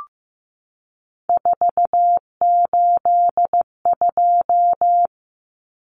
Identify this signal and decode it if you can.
Morse code, heard as a keyed tone, about 15 wpm, 700 Hz. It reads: 482
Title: Common Cut Numbers - 3-Digit 15wpm